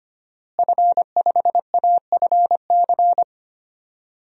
Morse code, heard as a keyed tone, about 25 wpm, 700 Hz. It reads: F5AFC
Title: DXCC-100 Call Signs - 5 Characters - Repeated 3x - 25wpm